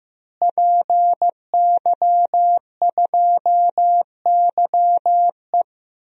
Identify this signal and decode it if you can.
Morse code, heard as a keyed tone, about 15 wpm, 700 Hz. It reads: PY2YE
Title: DXCC-100 Call Signs - 5 Characters - Repeated 2x - 15wpm